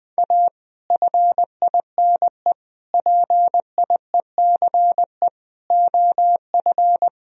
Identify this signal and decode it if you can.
Morse code, heard as a keyed tone, about 20 wpm, 700 Hz. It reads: A FINE PIECE OF
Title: Sentences from Top 1000 Words - Review Part 2 - 20wpm